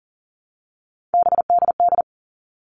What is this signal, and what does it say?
Morse code, heard as a keyed tone, about 40 wpm, 700 Hz. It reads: BDD
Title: All Cut Numbers - 3-Digit 40wpm